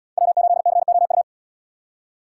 Morse code, heard as a keyed tone, about 76 wpm, 700 Hz. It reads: W8COU